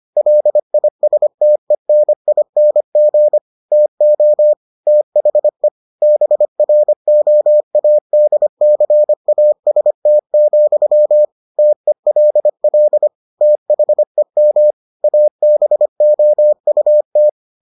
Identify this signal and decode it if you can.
Morse code, heard as a keyed tone, about 25 wpm, 600 Hz. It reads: LISTENING TO THE BROADCAST, TELL THEM ABOUT